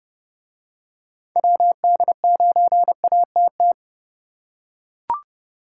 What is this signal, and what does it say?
Morse code, heard as a keyed tone, about 29 wpm, 700 Hz. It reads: WD9ATT